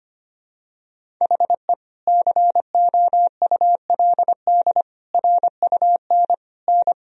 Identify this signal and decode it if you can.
Morse code, heard as a keyed tone, about 25 wpm, 700 Hz. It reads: HE COULD RUN N
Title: Sentences from Top 300 Words and Abbreviations 25wpm